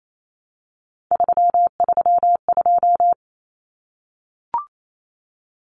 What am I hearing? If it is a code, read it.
Morse code, heard as a keyed tone, about 28 wpm, 700 Hz. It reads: 332
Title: Common Cut Numbers - 3-Digit 28wpm